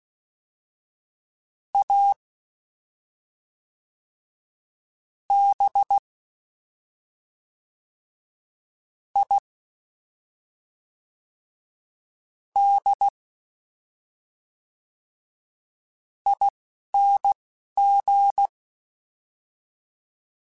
Morse code, heard as a keyed tone, about 16 wpm, spaced as for 4 wpm, 800 Hz. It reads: ABIDING
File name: Special INGs / Special INGs-mp3s / Fast ing @10